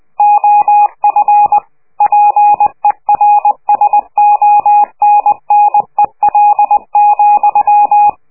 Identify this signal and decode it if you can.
Morse code, heard as a keyed tone, about 20 wpm, 830 Hz, with radio noise. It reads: OF PERSONNEL,
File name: worldcoastal / scheveningenradio / morsepch